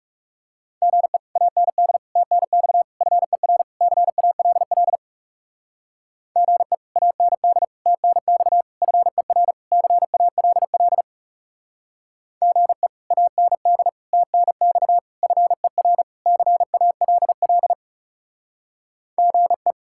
Morse code, heard as a keyed tone, about 35 wpm, 700 Hz. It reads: GE AND TNX FER CALL GE AND TNX FER CALL GE AND TNX FER CALL GE
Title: Speed-Racing Common QSO Phrases - 30wpm